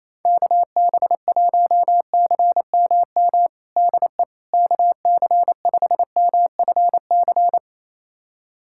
Morse code, heard as a keyed tone, about 27 wpm, 700 Hz. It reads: KB1CMM DE KC5MFC
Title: USA Callsign DE Callsign - 6 Characters - 28wpm